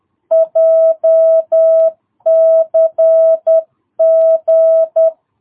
Morse code, heard as a keyed tone, about 10 wpm, 650 Hz. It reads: JCG
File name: jh8cbh / 2022cw